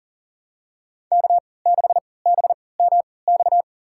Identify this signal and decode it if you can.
Morse code, heard as a keyed tone, about 40 wpm, spaced as for 23 wpm, 700 Hz. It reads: K6BMX
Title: USA Call Signs - 5 Characters - 3x Character Spacing - 40wpm